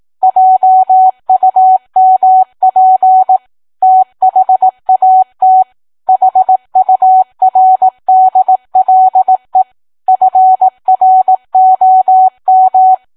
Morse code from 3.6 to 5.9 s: THAT